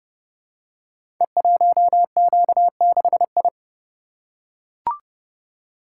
Morse code, heard as a keyed tone, about 30 wpm, 700 Hz. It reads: E1Q6I
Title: Code Groups - 5 Letter:Numbers - 30wpm